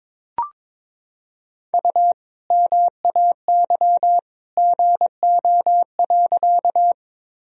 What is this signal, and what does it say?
Morse code, heard as a keyed tone, about 22 wpm, 700 Hz. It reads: U MAY GO.
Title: Sentences from Top 100 Words and Abbreviations 22wpm